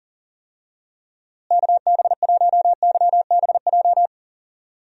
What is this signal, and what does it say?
Morse code, heard as a keyed tone, about 40 wpm, 700 Hz. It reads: KB1YBJ
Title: USA Call Signs - 6 Characters - Repeated 2x - 40wpm